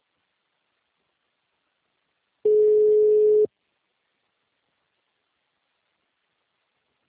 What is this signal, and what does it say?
Ringback tone (Europe, 425 Hz).